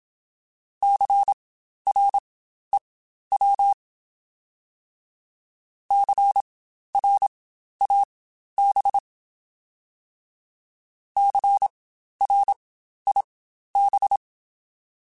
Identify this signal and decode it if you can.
Morse code, heard as a keyed tone, about 27 wpm, spaced as for 12 wpm, 800 Hz. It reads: CREW CRAB CRIB